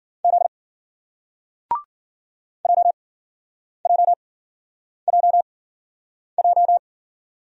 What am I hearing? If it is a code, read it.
Morse code, heard as a keyed tone, about 50 wpm, spaced as for 9 wpm, 700 Hz. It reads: Z JJJJ